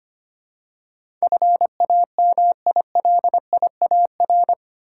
Morse code, heard as a keyed tone, about 25 wpm, 690 Hz. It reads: FAMILIAR
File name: 8 Letter Words - First 3 Letters Given - 25wpm